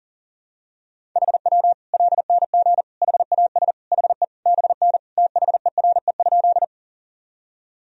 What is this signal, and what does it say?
Morse code, heard as a keyed tone, about 37 wpm, 700 Hz. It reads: HW LNG HAS HE BN THERE?
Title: Sentences from Top 100 Words and Abbreviations 40wpm